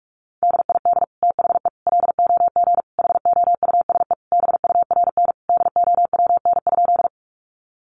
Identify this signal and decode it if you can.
Morse code, heard as a keyed tone, about 44 wpm, 700 Hz. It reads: DID THE LOG HOUSE BURN DOWN?
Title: Sentences from Top 1400 Words 45wpm